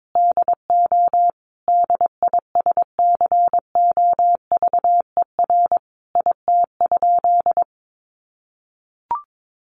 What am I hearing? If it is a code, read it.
Morse code, heard as a keyed tone, about 22 wpm, 700 Hz. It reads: DO DISCOVER IT?
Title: Sentences from Top 1100 Words 22wpm